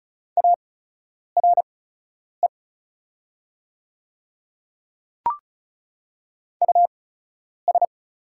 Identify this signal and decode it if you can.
Morse code, heard as a keyed tone, about 36 wpm, spaced as for 10 wpm, 700 Hz. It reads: ARE US
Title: 8x Character Spacing - Top 300 Words - 35wpm